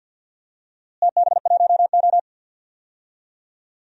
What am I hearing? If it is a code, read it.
Morse code, heard as a keyed tone, about 49 wpm, 700 Hz. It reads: TB1O
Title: DXCC-100 Call Signs - 4 Characters - Repeated 3x - 50wpm